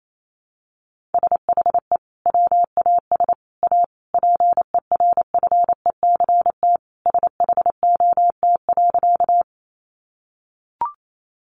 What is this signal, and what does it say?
Morse code, heard as a keyed tone, about 28 wpm, 700 Hz. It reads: SHE WAS A PERFECT SHOT.
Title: Sentences from Top 1100 Words 28wpm